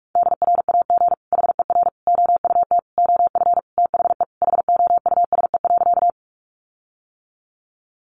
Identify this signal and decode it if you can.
Morse code, heard as a keyed tone, about 44 wpm, 700 Hz. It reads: DRAG HER OUT OF THE HOUSE.
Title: Sentences from Top 1500 Words 45wpm